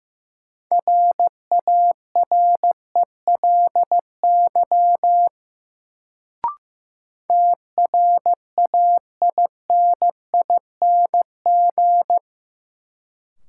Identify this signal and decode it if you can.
Morse code, heard as a keyed tone, about 16 wpm, 700 Hz. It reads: RARELY TRAINING
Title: Sentences from Top 1400 Words - New Word Drill Alt - 15wpm